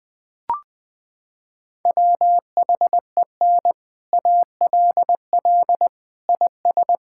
Morse code, heard as a keyed tone, about 21 wpm, 700 Hz. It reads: WHEN ALL IS
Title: Sets of 6 Words - 20wpm